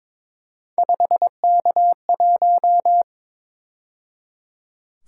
Morse code, heard as a keyed tone, about 26 wpm, 700 Hz. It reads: 5K1
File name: Speed-Racing - Three Letter-Number Combinations - 22wpm